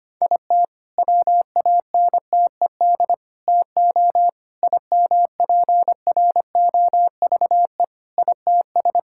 Morse code, heard as a keyed tone, about 25 wpm, 700 Hz. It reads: IT WANTED TO IMPROVE ITS
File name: practice / 25wpm / All-Age Appropriate Jokes - 25wpm